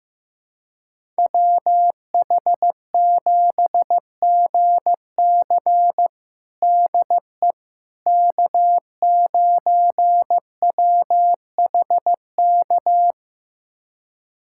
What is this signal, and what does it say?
Morse code, heard as a keyed tone, about 15 wpm, 700 Hz. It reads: WH7GC DE K9WHK
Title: USA Callsign DE Callsign - 5 Characters - 15wpm